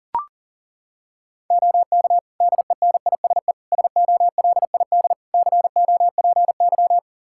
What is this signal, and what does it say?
Morse code, heard as a keyed tone, about 40 wpm, 700 Hz. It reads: OK DENISE SOLID COPY
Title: K1USN SST QSO Practice 40wpm